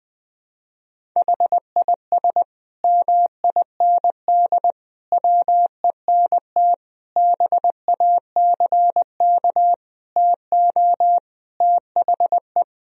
Morse code, heard as a keyed tone, about 20 wpm, 700 Hz. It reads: HIS MIND WENT BACK TO THE
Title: Sentences from Top 1000 Words - Review Part 3 - 20wpm